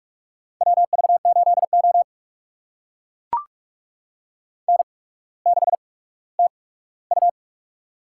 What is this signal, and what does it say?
Morse code, heard as a keyed tone, about 44 wpm, spaced as for 19 wpm, 690 Hz. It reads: WV8O N6TU
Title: USA Call Signs - 4 Characters - 8x Character Spacing - 45wpm